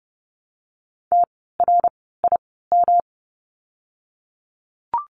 Morse code, heard as a keyed tone, about 30 wpm, spaced as for 17 wpm, 700 Hz. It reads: TRIM
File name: course / 30:10wpm / 108 Lesson 18f - Letter m - Words 30wpm